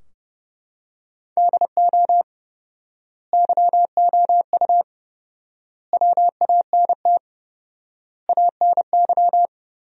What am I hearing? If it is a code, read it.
Morse code, heard as a keyed tone, about 30 wpm, 700 Hz. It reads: DO YOU WANT ANY
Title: 4x Word-Spacing - Sentences from Top 400 Words - 30wpm